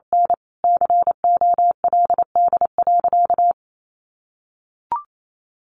Morse code, heard as a keyed tone, about 28 wpm, 700 Hz. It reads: N COLD.
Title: Sentences from Top 1100 Words 28wpm